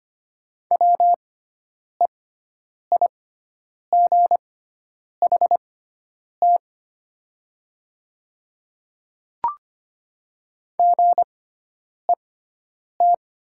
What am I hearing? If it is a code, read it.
Morse code, heard as a keyed tone, about 25 wpm, spaced as for 9 wpm, 700 Hz. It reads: WEIGHT GET
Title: 6x Character Spacing - Top 500 Words - 25wpm